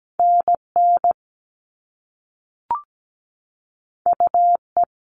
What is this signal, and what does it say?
Morse code, heard as a keyed tone, about 17 wpm, 700 Hz. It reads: NN UE
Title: All Cut Numbers - 2-Digit 17wpm